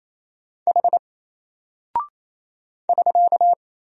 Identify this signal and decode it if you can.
Morse code, heard as a keyed tone, about 27 wpm, 700 Hz. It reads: H <SK>
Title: Single Character (Letter:Number:Punctuation:Prosign) 28wpm